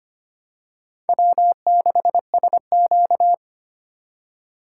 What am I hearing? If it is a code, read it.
Morse code, heard as a keyed tone, about 25 wpm, 700 Hz. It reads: W6SQ
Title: USA Call Signs - 4 Characters - 2x Character Spacing - 25wpm